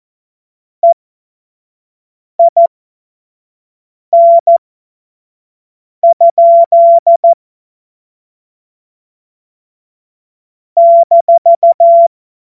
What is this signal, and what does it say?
Morse code, heard as a keyed tone, about 14 wpm, spaced as for 5 wpm, 680 Hz. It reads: EIN? -
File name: Morsekurs / Witze / Witz 17. Januar 2023 25 F70